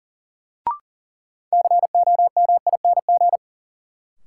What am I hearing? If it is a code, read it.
Morse code, heard as a keyed tone, about 40 wpm, 700 Hz. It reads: COMING
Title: Sentences from Top 1300 Words - New Word Drill Alt - 40wpm